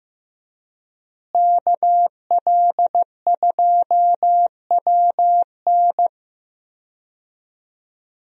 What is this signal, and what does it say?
Morse code, heard as a keyed tone, about 15 wpm, 700 Hz. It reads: KL2WN
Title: USA Call Signs - 5 Characters - Repeated 3x - 15wpm